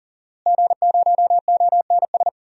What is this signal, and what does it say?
Morse code, heard as a keyed tone, about 40 wpm, 700 Hz. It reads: G0ONS
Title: DXCC-100 Call Signs - 5 Characters - Repeated 3x - 40wpm